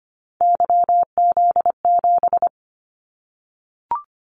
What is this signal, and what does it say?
Morse code, heard as a keyed tone, about 25 wpm, 700 Hz. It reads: YZ7
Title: Callsign Prefixes for Top 25 DXCC - 25wpm